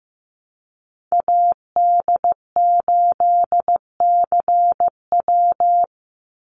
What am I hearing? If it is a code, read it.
Morse code, heard as a keyed tone, about 15 wpm, 700 Hz. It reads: AD8CW
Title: Call Signs Repeated 3x - 15wpm